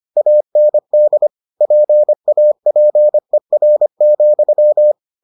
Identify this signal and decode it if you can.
Morse code, heard as a keyed tone, about 25 wpm, 600 Hz. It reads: AND PAPER,